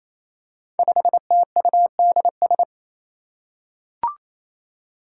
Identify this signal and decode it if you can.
Morse code, heard as a keyed tone, about 28 wpm, 700 Hz. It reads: 5TUDS
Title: Five Challenging Letter-Number Combinations 28wpm